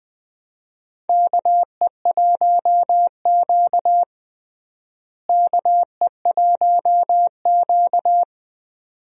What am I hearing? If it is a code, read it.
Morse code, heard as a keyed tone, about 20 wpm, 700 Hz. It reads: KE1Q KE1Q
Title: USA Call Signs - 4 Characters - Repeated 2x - 20wpm